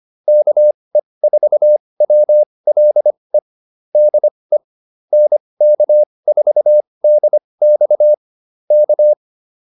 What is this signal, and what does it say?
Morse code, heard as a keyed tone, about 25 wpm, spaced as for 20 wpm, 600 Hz. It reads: KE4WLE DE NK4DX K